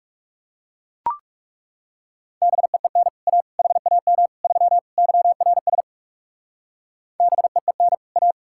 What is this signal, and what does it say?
Morse code, heard as a keyed tone, about 38 wpm, 700 Hz. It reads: BEEN A HAM 3 YRS BEEN A